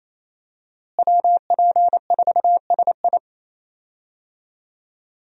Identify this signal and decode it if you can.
Morse code, heard as a keyed tone, about 27 wpm, 700 Hz. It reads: WP4SI
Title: USA Call Signs - 5 Characters - Repeated 3x - 28wpm